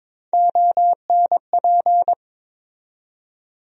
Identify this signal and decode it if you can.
Morse code, heard as a keyed tone, about 22 wpm, 700 Hz. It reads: ONP
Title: USA Call Sign Postfixes - 3 Letters 22wpm